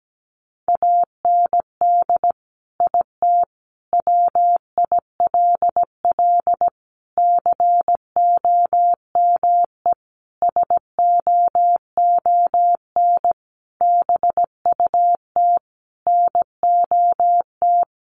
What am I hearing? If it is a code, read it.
Morse code, heard as a keyed tone, about 17 wpm, 700 Hz. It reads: AND IT WILL COME SOON BUT NOT
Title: Sentences from Top 1100 Words 17wpm